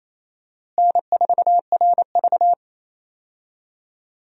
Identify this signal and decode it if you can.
Morse code, heard as a keyed tone, about 28 wpm, 700 Hz. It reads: N4RV
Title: USA Call Signs - 4 Characters - Repeated 3x - 28wpm